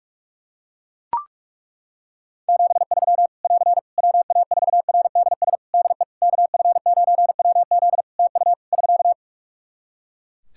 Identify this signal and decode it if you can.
Morse code, heard as a keyed tone, about 41 wpm, 700 Hz. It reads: 73 <AR> WA4RDS DE KF9WZ TU <SK>
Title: K1USN SST QSO Practice 45wpm